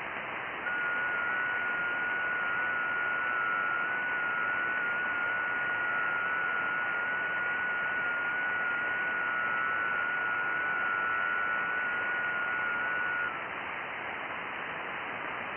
FT8.
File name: IC-7100